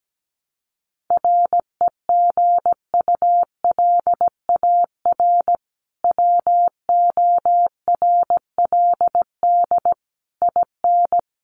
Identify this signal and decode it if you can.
Morse code, heard as a keyed tone, about 17 wpm, 700 Hz. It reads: REGULAR WORLD IN
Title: Sentences from Top 900 Words 17wpm